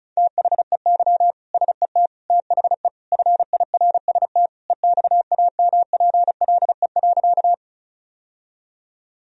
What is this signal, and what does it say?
Morse code, heard as a keyed tone, about 36 wpm, 700 Hz. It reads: THEY SET THE FIRST EXAMPLE.
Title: Sentences from Top 1000 Words - Review Part 1 - 35wpm